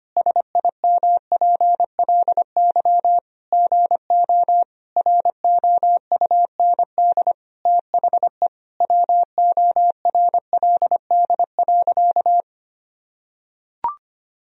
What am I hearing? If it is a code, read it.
Morse code, heard as a keyed tone, about 25 wpm, 700 Hz. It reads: SIMPLY GO ROUND THE WORLD.
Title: Sentences from Top 1000 Words - Review Part 4 - 25wpm